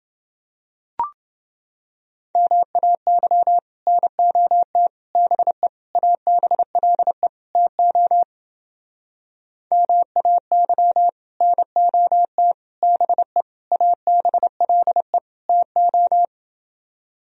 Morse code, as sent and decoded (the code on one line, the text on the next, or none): -- .- -.-- / -. --- - / -... / .- -... .-.. / - --- / -- .- -.-- / -. --- - / -... / .- -... .-.. / - ---
MAY NOT BE ABLE TO MAY NOT BE ABLE TO